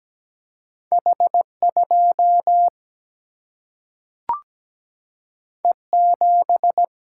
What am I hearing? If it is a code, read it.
Morse code, heard as a keyed tone, about 17 wpm, 700 Hz. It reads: H 2 E 7
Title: Single-Letter Single-Number 17wpm